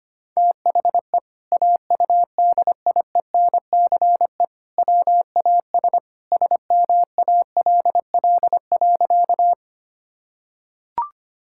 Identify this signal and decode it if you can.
Morse code, heard as a keyed tone, about 25 wpm, 700 Hz. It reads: THE AUDIENCE WAS SMALL.
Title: Sentences from Top 1000 Words - Review Part 5 - 25wpm